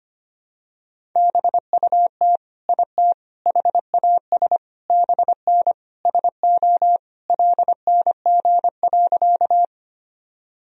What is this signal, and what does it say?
Morse code, heard as a keyed tone, about 25 wpm, 700 Hz. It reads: BUT IT HAS BN SO LNG.
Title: Sentences from Top 100 Words and Abbreviations 25wpm